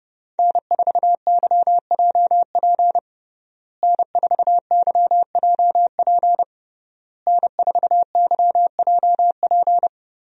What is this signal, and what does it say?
Morse code, heard as a keyed tone, about 30 wpm, 700 Hz. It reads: N4YJP N4YJP N4YJP
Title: USA Call Signs - 5 Characters - Repeated 3x - 30wpm